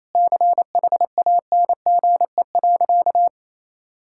Morse code, heard as a keyed tone, about 28 wpm, 700 Hz. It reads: CHANGE.
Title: Sentences from Top 1000 Words - Review Part 5 - 28wpm